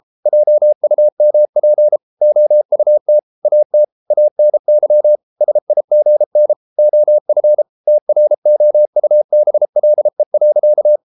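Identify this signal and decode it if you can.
Morse code, heard as a keyed tone, about 33 wpm, 600 Hz. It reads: JUMP OUT AT ANY SIGN OF TROUBLE.
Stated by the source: Morse code practice at 32 wpm from CW med Gnister, reading Ken Follett's novel Never in English.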